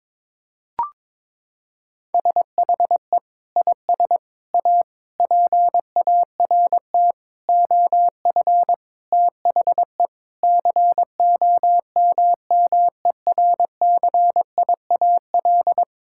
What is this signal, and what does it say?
Morse code, heard as a keyed tone, about 22 wpm, 700 Hz. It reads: SHE IS A PART OF THE COMMERCIAL
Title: Sentences from Top 1000 Words - Review Part 5 - 22wpm